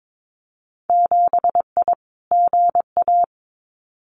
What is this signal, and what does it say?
Morse code, heard as a keyed tone, about 22 wpm, 700 Hz. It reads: 7I GA
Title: Winter Field Day Exchanges Encore 22wpm